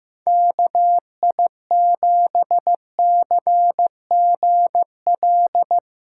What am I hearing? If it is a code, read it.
Morse code, heard as a keyed tone, about 16 wpm, 700 Hz. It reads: KI7CGL
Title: USA Callsign DE Callsign - 6 Characters - 15wpm